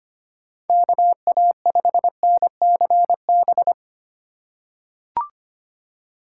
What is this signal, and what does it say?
Morse code, heard as a keyed tone, about 25 wpm, 700 Hz. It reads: KA5NCB
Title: USA Call Signs - 6 Characters - 2x Character Spacing - 25wpm